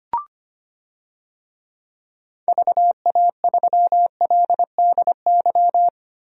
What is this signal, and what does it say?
Morse code, heard as a keyed tone, about 25 wpm, 700 Hz. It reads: VA3LDY
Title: DXCC-100 Call Signs - 6 Characters - 25wpm